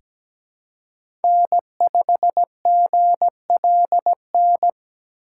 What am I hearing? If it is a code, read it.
Morse code, heard as a keyed tone, about 17 wpm, 700 Hz. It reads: N5GLN
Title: USA Call Signs - 5 Characters - 4x Character Spacing - 17wpm